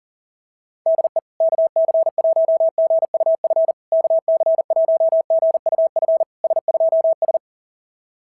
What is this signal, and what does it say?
Morse code, heard as a keyed tone, about 37 wpm, 660 Hz. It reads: DE KC1GUF KC1GUF S2S
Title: SOTA QSO Practice 40wpm